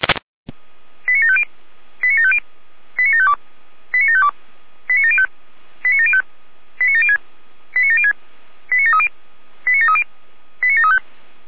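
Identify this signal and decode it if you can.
Five-tone selective call (ZVEI), heard as quick runs of five tones, tones of about 70 ms. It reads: 89750 89750 89752 89752 89705 89705 89706 89706 89730 89730 89735
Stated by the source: ZVEI five-tone selective call received on SDR.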